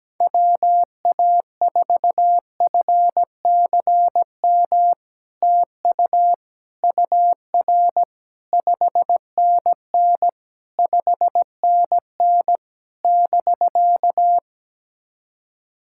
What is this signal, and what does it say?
Morse code, heard as a keyed tone, about 17 wpm, 700 Hz. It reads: WA4FCM TU UR 5NN 5NN <BK>
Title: SOTA QSO Practice 17wpm